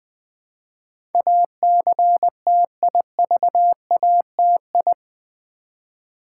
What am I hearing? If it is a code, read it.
Morse code, heard as a keyed tone, about 20 wpm, 700 Hz. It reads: ACTIVATI
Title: Build A Word - 4 Letters At A Time - 10-Letter Words - 20wpm